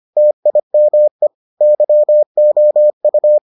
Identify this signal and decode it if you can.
Morse code, heard as a keyed tone, about 25 wpm, 600 Hz. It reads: TIME YOU